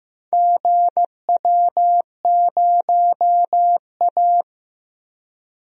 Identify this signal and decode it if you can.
Morse code, heard as a keyed tone, about 15 wpm, 700 Hz. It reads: GW0A
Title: DXCC-100 Call Signs - 4 Characters - Repeated 2x - 15wpm